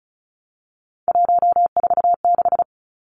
Morse code, heard as a keyed tone, about 34 wpm, 700 Hz. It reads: 146
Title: Numbers - 3-Digit 35wpm